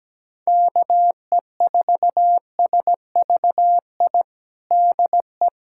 Morse code, heard as a keyed tone, about 17 wpm, 700 Hz. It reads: KE4SVI DE
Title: USA Callsign DE Callsign - 6 Characters - 17wpm